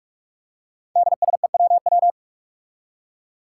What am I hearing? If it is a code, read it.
Morse code, heard as a keyed tone, about 46 wpm, 700 Hz. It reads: DSEWW
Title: Five Challenging Letter-Number Combinations 45wpm